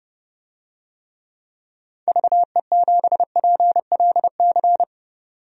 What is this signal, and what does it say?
Morse code, heard as a keyed tone, about 31 wpm, 700 Hz. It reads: VE7PLC
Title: DXCC-100 Call Signs - 6 Characters - 30wpm